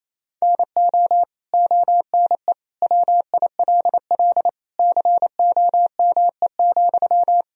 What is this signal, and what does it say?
Morse code, heard as a keyed tone, about 27 wpm, 700 Hz. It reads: NO ONE WILL COME,
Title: Sentences from Top 100 Words and Abbreviations 28wpm